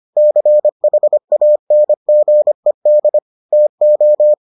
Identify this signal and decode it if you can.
Morse code, heard as a keyed tone, about 25 wpm, 600 Hz. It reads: CHANGED TO